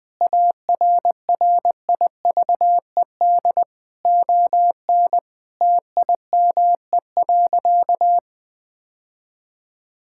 Morse code, heard as a keyed tone, about 20 wpm, 700 Hz. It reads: ARRIVED ON TIME.
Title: Sentences with 2-Word Phrases - 20wpm